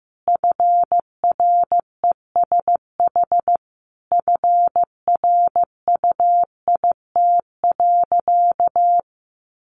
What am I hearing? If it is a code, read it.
Morse code, heard as a keyed tone, about 15 wpm, 700 Hz. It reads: FRESH FRUIT.
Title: Sentences from Top 1300 Words 15wpm